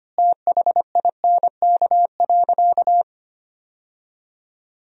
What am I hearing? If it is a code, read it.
Morse code, heard as a keyed tone, about 25 wpm, 700 Hz. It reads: THINK.
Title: 4x Word-Spacing - Sentences from Top 500 Words - 25wpm